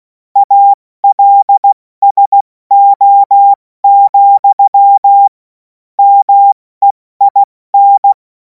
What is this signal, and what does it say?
Morse code, heard as a keyed tone, about 16 wpm, 800 Hz. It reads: ALSO, MEIN
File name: Morsekurs / Witze / Witz 24. März 2026 70 F80